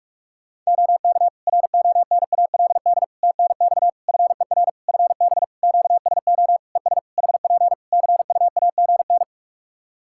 Morse code, heard as a keyed tone, about 46 wpm, 700 Hz. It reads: OK RONALD TNX FER FB QSO ES HP CUAGN
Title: Practice QSO - Exchange 4 - Ending - 45wpm